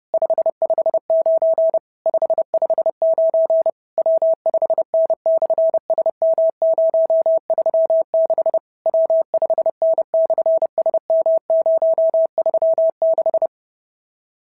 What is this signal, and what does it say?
Morse code, heard as a keyed tone, about 30 wpm, 650 Hz. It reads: 559 559 W5N/SM036 W5N/SM036
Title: SOTA QSO Practice 30wpm